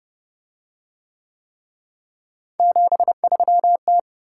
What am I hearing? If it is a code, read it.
Morse code, heard as a keyed tone, about 28 wpm, 700 Hz. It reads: 73T